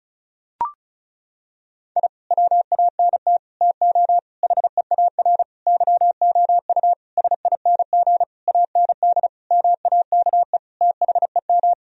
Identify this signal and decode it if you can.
Morse code, heard as a keyed tone, about 36 wpm, 700 Hz. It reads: I WANT TO HEAR YOU SING AND MAKE THEM
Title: Sentences from Top 1000 Words - Review Part 4 - 35wpm